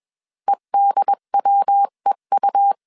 Morse code, heard as a keyed tone, about 21 wpm, 800 Hz. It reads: EDWEU
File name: CW / DET / 1 3 1500C S B 20wpm 1754